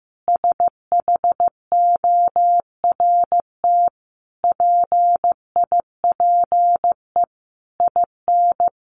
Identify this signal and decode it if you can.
Morse code, heard as a keyed tone, about 15 wpm, 700 Hz. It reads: SHORT PIPE IN
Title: Sentences from Top 1500 Words 15wpm